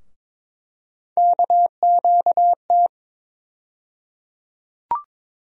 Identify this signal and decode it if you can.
Morse code, heard as a keyed tone, about 22 wpm, 700 Hz. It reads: KQT
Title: USA Call Sign Postfixes - 3 Letters 22wpm